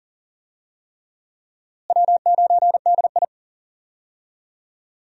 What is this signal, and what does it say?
Morse code, heard as a keyed tone, about 40 wpm, 700 Hz. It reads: W9DI
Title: USA Call Signs - 4 Characters - Repeated 3x - 40wpm